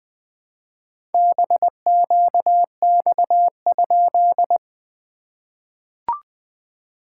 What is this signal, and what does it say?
Morse code, heard as a keyed tone, about 20 wpm, 700 Hz. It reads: BQX?
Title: SOTA QSO Practice 20wpm